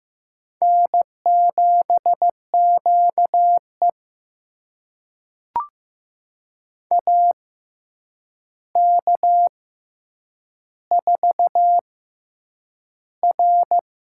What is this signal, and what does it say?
Morse code, heard as a keyed tone, about 15 wpm, spaced as for 8 wpm, 700 Hz. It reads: N7QE AK4R
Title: USA Call Signs - 4 Characters - 6x Character Spacing - 15wpm